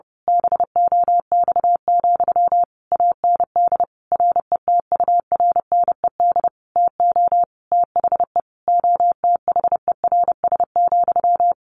Morse code, heard as a keyed tone, about 30 wpm, 700 Hz. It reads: BOX, AND RETURNED TO THE OTHERS,
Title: A Sound Of Thunder 30wpm